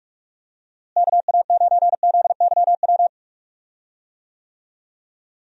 Morse code, heard as a keyed tone, about 46 wpm, 700 Hz. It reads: KA9ZYW
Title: USA Call Signs - 6 Characters - Repeated 3x - 45wpm